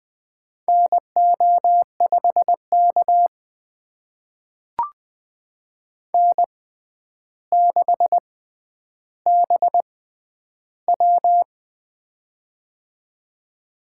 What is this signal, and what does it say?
Morse code, heard as a keyed tone, about 20 wpm, spaced as for 10 wpm, 700 Hz. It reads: NO5K N6BW
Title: USA Call Signs - 4 Characters - 6x Character Spacing - 20wpm